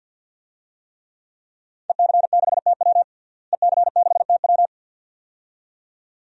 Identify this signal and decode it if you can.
Morse code, heard as a keyed tone, about 53 wpm, 700 Hz. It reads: EC6TW EC6TW
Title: DXCC-100 Call Signs - 5 Characters - Repeated 2x - 50wpm